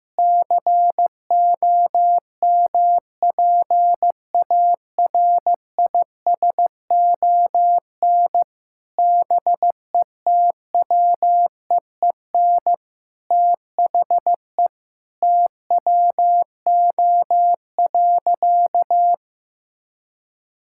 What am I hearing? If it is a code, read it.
Morse code, heard as a keyed tone, about 15 wpm, 700 Hz. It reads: COMPARISON BETWEEN THE TWO.
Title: Sentences from Top 1600 Words 15wpm